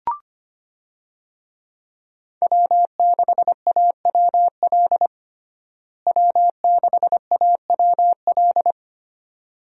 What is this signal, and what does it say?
Morse code, heard as a keyed tone, about 25 wpm, 700 Hz. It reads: W6AWL W6AWL
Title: USA Call Signs - 5 Characters - Repeated 2x - 25wpm